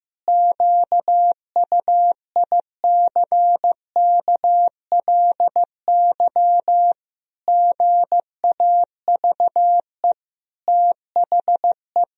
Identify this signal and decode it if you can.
Morse code, heard as a keyed tone, about 15 wpm, 700 Hz. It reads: QUICKLY GAVE THE
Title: Sentences from Top 1000 Words - Review Part 4 - 15wpm